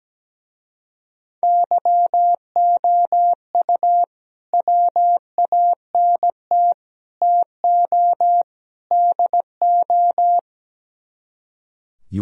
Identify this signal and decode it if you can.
Morse code, heard as a keyed tone, about 17 wpm, 700 Hz. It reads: YOU WANT TO DO
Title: Speed-Racing - Sets of 4 Words 17wpm